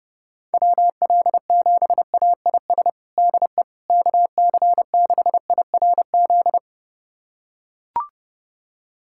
Morse code, heard as a keyed tone, about 30 wpm, 700 Hz. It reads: WL7AIS DE KC6IRZ
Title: USA Callsign DE Callsign 30wpm